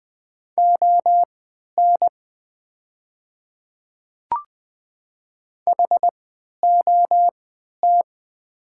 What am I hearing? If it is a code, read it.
Morse code, heard as a keyed tone, about 20 wpm, spaced as for 11 wpm, 700 Hz. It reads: ON HOT
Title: 3x Character Spacing - Top 100 Words - 20wpm